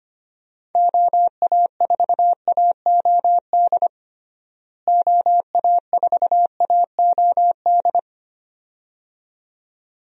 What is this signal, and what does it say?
Morse code, heard as a keyed tone, about 25 wpm, 700 Hz. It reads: OA4AOD OA4AOD
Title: USA Call Signs - 6 Characters - Repeated 2x - 25wpm